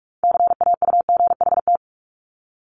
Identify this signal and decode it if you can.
Morse code, heard as a keyed tone, about 46 wpm, 700 Hz. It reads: CAUGHT
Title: Multi-letter Phonemes Repeated 3x 45wpm